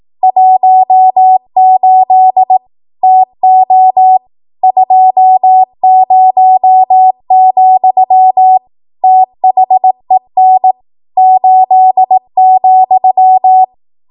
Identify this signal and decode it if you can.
Morse code, heard as a keyed tone, about 18 wpm, 750 Hz. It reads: 18 TO 20, THEN 8,
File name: Archive / 18 WPM / 18WPM